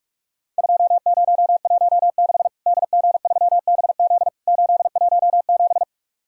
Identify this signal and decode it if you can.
Morse code, heard as a keyed tone, about 46 wpm, 700 Hz. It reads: 2016 DG3BZ 817
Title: Worked All Europe DX Contest - QTC Exchange - 45wpm